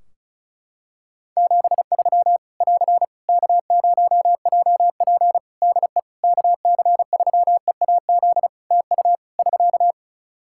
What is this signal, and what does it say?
Morse code, heard as a keyed tone, about 36 wpm, 700 Hz. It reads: 73 <AR> K0JP DE KC3EAZ TU <SK>
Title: Practice QSO - Exchange 4 - Ending - 35wpm